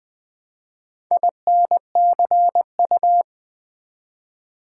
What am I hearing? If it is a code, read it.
Morse code, heard as a keyed tone, about 20 wpm, 700 Hz. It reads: INCU
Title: Build A Word - 1 Letter At A Time - 5-Letter Words - 20wpm